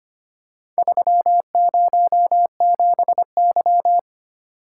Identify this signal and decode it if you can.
Morse code, heard as a keyed tone, about 25 wpm, 700 Hz. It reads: 307Y
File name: practice / 25wpm / Four Letter-Number Combinations - 25wpm